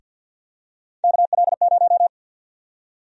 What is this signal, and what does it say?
Morse code, heard as a keyed tone, about 50 wpm, 700 Hz. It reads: KL0
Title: USA Call Sign Prefixes - Repeated 2x - 50wpm